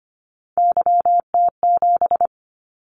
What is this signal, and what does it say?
Morse code, heard as a keyed tone, about 25 wpm, 700 Hz. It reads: YT7
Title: Callsign Prefixes for Top 25 DXCC - 25wpm